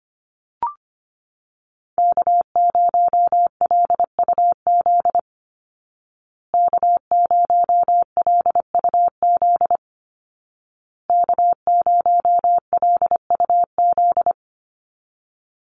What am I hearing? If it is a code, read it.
Morse code, heard as a keyed tone, about 25 wpm, 700 Hz. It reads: K0LUZ K0LUZ K0LUZ
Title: Call Signs Repeated 2x - 25wpm